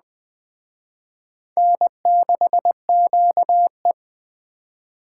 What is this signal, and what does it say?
Morse code, heard as a keyed tone, about 20 wpm, 700 Hz. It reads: N6QE
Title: Speed Racing - USA Call Signs 17wpm